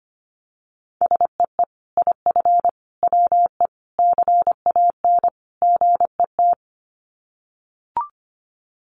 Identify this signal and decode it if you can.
Morse code, heard as a keyed tone, about 25 wpm, 700 Hz. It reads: SEE IF WE CAN GET
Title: Sets of 5 Words Encore2 25wpm